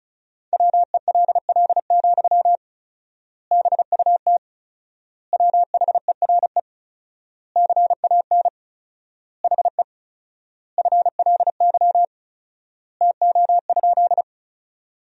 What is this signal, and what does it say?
Morse code, heard as a keyed tone, about 36 wpm, 700 Hz. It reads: WELL, BUT WHERE CAN HE FLY TO?
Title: 4x Word-Spacing - Sentences from Top 500 Words - 35wpm